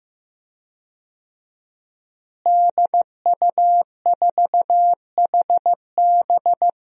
Morse code, heard as a keyed tone, about 15 wpm, 700 Hz. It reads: DU4HB
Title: DXCC-100 Call Signs - 5 Characters - Repeated 3x - 15wpm